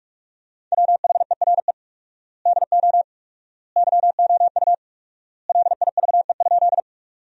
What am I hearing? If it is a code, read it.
Morse code, heard as a keyed tone, about 46 wpm, 700 Hz. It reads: WHERE DO YOU LIVE?
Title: 4x Word-Spacing - Sentences from Top 200 Words - 45wpm